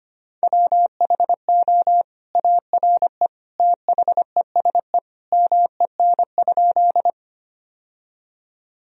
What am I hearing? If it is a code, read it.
Morse code, heard as a keyed tone, about 25 wpm, 700 Hz. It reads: WHO ARE THESE MEN?
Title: Sentences from Top 1000 Words - Review Part 1 - 25wpm